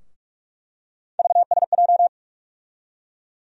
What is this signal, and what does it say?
Morse code, heard as a keyed tone, about 46 wpm, 700 Hz. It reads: VSJ
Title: USA Call Sign Postfixes - 3 Letters 45wpm